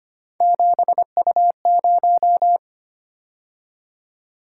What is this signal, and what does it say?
Morse code, heard as a keyed tone, about 25 wpm, 700 Hz. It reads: 7U0
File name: Three Letter-Number Combinations - 25wpm